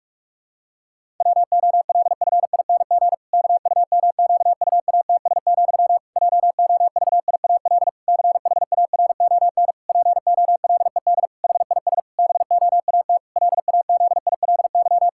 Morse code, heard as a keyed tone, about 46 wpm, 700 Hz. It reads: WOLFING KUMQUATS, JOVIAL CHARON POLED HIS BOAT LAZILY
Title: Pangram Challenge v2 - 45wpm